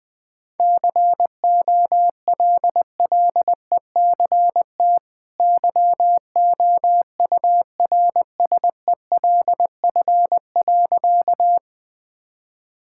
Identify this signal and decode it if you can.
Morse code, heard as a keyed tone, about 20 wpm, 700 Hz. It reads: COLLECT YOURSELF.
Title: Sentences from Top 1600 Words 20wpm